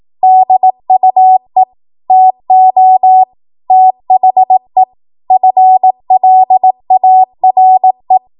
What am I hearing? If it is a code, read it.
Morse code, heard as a keyed tone, about 18 wpm, 750 Hz. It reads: DUE TO THE FLARE
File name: Morse / 18WPM